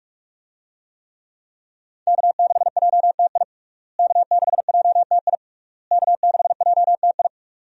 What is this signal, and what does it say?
Morse code, heard as a keyed tone, about 46 wpm, 700 Hz. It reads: K6JTI K6JTI K6JTI
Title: USA Call Signs - 5 Characters - Repeated 3x - 45wpm